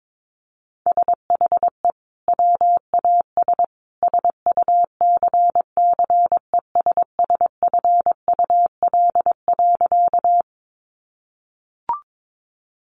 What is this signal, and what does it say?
Morse code, heard as a keyed tone, about 22 wpm, 700 Hz. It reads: SHE WAS SUCCESSFUL.
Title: Sentences from Top 900 Words 22wpm